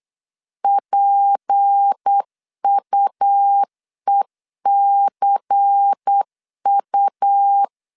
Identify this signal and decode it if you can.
Morse code, heard as a keyed tone, about 8 wpm, 800 Hz. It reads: PUECU
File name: LEZIONI / 6 PKX0 / 6 P 8wpm 3025